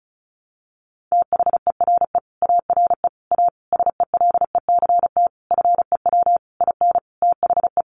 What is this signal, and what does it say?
Morse code, heard as a keyed tone, about 34 wpm, 700 Hz. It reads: THERE ARE A SELECT FEW IN THE
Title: Sentences from Top 1300 Words 35wpm